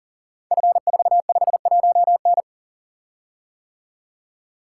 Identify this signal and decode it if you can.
Morse code, heard as a keyed tone, about 42 wpm, 690 Hz. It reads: F451N
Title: Code Groups - 5 Letter:Numbers - 40wpm